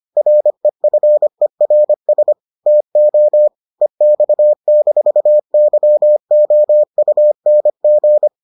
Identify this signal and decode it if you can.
Morse code, heard as a keyed tone, about 25 wpm, 600 Hz. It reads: REFERS TO EX-YOUNG